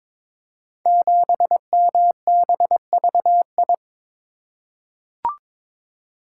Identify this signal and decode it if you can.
Morse code, heard as a keyed tone, about 22 wpm, 700 Hz. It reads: 7MBVI
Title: Code Groups - 5 Letter:Numbers - 22wpm